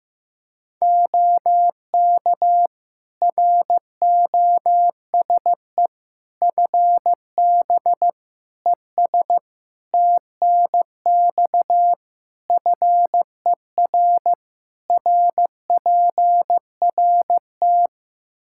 Morse code, heard as a keyed tone, about 15 wpm, 700 Hz. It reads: OK ROSE FB ES TNX FER RPRT
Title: Practice QSO - Exchange 2 - Info - 15wpm